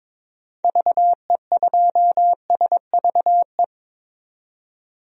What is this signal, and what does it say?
Morse code, heard as a keyed tone, about 22 wpm, 700 Hz. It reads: VE2SVE
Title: DXCC-100 Call Signs - 6 Characters - 22wpm